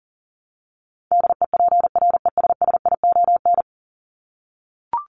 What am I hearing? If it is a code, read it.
Morse code, heard as a keyed tone, about 40 wpm, 700 Hz. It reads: DEPRESSION
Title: Top 1001-1250 Words 40wpm